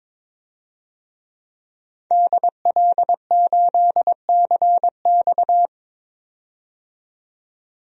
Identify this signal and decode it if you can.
Morse code, heard as a keyed tone, about 22 wpm, 700 Hz. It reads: DL8CX
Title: DXCC-100 Call Signs - 5 Characters - 22wpm